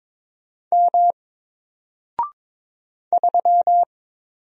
Morse code, heard as a keyed tone, about 23 wpm, 700 Hz. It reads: M 3
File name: Single-Letter-Number-22wpm